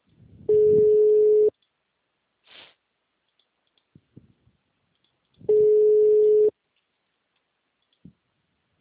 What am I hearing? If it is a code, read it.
Ringback tone (Europe, 425 Hz).